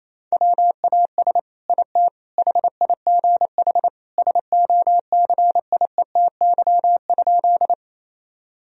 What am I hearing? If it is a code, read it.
Morse code, heard as a keyed tone, about 28 wpm, 700 Hz. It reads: WAS IT HIGH SOCIETY?
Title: Sentences from Top 1000 Words - Review Part 4 - 28wpm